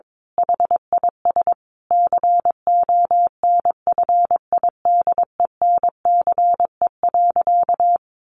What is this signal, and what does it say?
Morse code, heard as a keyed tone, about 22 wpm, 700 Hz. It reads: HIS CONFIDENCE.
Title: Sentences from Top 1500 Words 22wpm